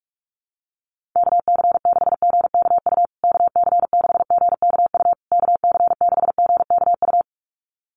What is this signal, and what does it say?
Morse code, heard as a keyed tone, about 44 wpm, 700 Hz. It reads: KC6GKU KC6GKU KC6GKU
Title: USA Call Signs Repeated 3x 45wpm